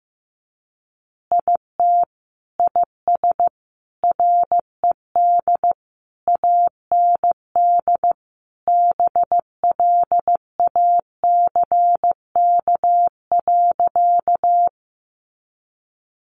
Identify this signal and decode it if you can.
Morse code, heard as a keyed tone, about 15 wpm, 700 Hz. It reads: IT IS RED AND BLACK.
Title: Easy Phrases 15wpm